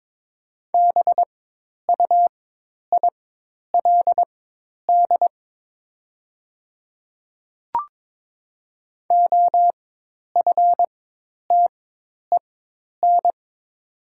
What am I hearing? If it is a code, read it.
Morse code, heard as a keyed tone, about 22 wpm, spaced as for 10 wpm, 700 Hz. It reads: BUILD OFTEN
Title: 4x Character Spacing - Top 300 Words - 22wpm